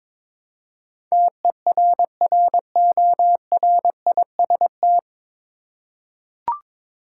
Morse code, heard as a keyed tone, about 22 wpm, 700 Hz. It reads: TERRORIST
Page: Morse Code Ninja practice 22wpm